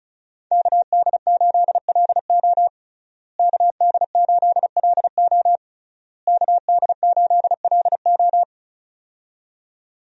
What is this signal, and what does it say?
Morse code, heard as a keyed tone, about 34 wpm, 700 Hz. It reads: KD8LO KD8LO KD8LO
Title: USA Call Signs - 5 Characters - Repeated 3x - 35wpm